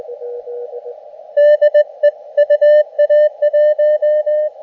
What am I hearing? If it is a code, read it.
Morse code, heard as a keyed tone, about 19 wpm, 600 Hz, with radio noise. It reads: DE UA1
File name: Expedition / SWL UT5NM 20.12.2012 VANGUARD UA1CEG p IT9IFI